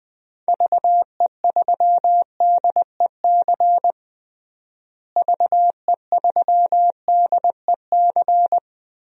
Morse code, heard as a keyed tone, about 20 wpm, 700 Hz. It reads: VE3DEC VE3DEC
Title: DXCC-100 Call Signs - 6 Characters - Repeated 3x - 20wpm